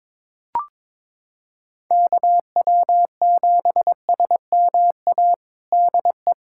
Morse code, heard as a keyed tone, about 21 wpm, 700 Hz. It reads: KW7SMA DE
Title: USA Callsign DE Callsign 22wpm